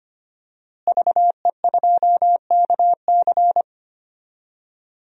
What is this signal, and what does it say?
Morse code, heard as a keyed tone, about 25 wpm, 700 Hz. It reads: VE2KC
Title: DXCC-100 Call Signs - 5 Characters - Repeated 3x - 25wpm